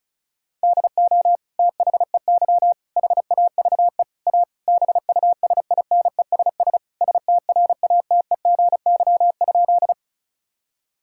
Morse code, heard as a keyed tone, about 34 wpm, 700 Hz. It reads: DO THEY HAVE A BUSINESS STRATEGY?
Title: Sentences from Top 1000 Words - Review Part 5 - 35wpm